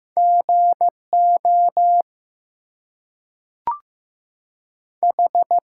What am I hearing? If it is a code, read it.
Morse code, heard as a keyed tone, about 15 wpm, 700 Hz. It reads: GO H
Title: Two Letter Words - 15wpm